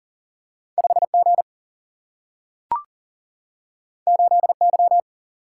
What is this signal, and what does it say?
Morse code, heard as a keyed tone, about 40 wpm, 700 Hz. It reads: 5 G 8 Y
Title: Single-Letter Single-Number 40wpm